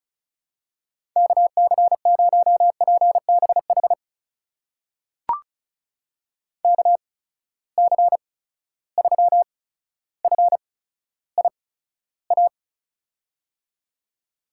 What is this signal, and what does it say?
Morse code, heard as a keyed tone, about 36 wpm, spaced as for 15 wpm, 700 Hz. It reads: KC0PBH KC3FIA
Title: USA Call Signs - 6 Characters - 8x Character Spacing - 35wpm